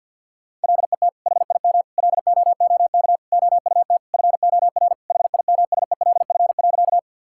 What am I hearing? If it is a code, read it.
Morse code, heard as a keyed tone, about 50 wpm, 700 Hz. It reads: LET HIM LOOK OUT FOR HIMSELF.
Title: Sentences from Top 1600 Words 50wpm